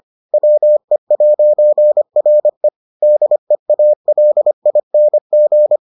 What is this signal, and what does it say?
Morse code, heard as a keyed tone, about 25 wpm, 600 Hz. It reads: WE'RE DEALING